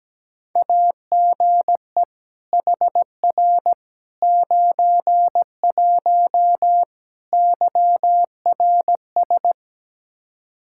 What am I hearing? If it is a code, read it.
Morse code, heard as a keyed tone, about 17 wpm, 700 Hz. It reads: AGE HR 91 YRS
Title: Practice QSO - Exchange 3 - More Info - 17wpm